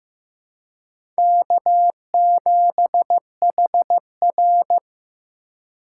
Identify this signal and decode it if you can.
Morse code, heard as a keyed tone, about 15 wpm, 700 Hz. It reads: K7HR
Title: USA Call Signs - 4 Characters - 3x Character Spacing - 15wpm